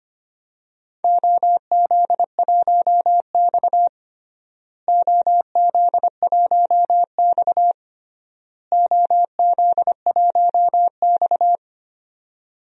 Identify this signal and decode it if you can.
Morse code, heard as a keyed tone, about 25 wpm, 700 Hz. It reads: OZ1X OZ1X OZ1X
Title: DXCC-100 Call Signs - 4 Characters - Repeated 3x - 25wpm